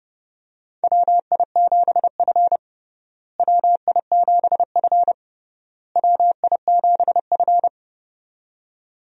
Morse code, heard as a keyed tone, about 32 wpm, 700 Hz. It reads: WI7F WI7F WI7F
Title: USA Call Signs - 4 Characters - Repeated 3x - 30wpm